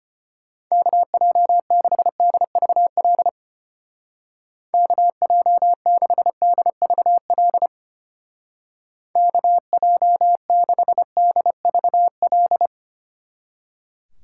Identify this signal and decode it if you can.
Morse code, heard as a keyed tone, about 32 wpm, 700 Hz. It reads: KJ6DVL KJ6DVL KJ6DVL